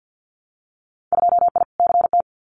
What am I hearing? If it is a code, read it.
Morse code, heard as a keyed tone, about 50 wpm, 700 Hz. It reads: 2I CT
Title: Winter Field Day Exchanges 50wpm